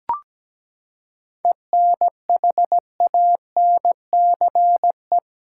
Morse code, heard as a keyed tone, about 17 wpm, 700 Hz. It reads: ENHANCE
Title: Seven Letter Words - 17wpm